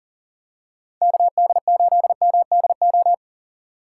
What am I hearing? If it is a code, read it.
Morse code, heard as a keyed tone, about 40 wpm, 700 Hz. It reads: KD8MDO
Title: USA Call Signs - 3x Character Spacing - 40wpm